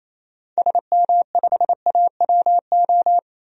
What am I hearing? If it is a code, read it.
Morse code, heard as a keyed tone, about 28 wpm, 700 Hz. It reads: SM5AWO
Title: DXCC-100 Call Signs - 6 Characters - 28wpm